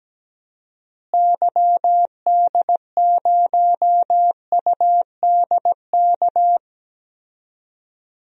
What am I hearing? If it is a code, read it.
Morse code, heard as a keyed tone, about 17 wpm, 700 Hz. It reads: YD0UDK
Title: DXCC-100 Call Signs - 6 Characters - Repeated 3x - 17wpm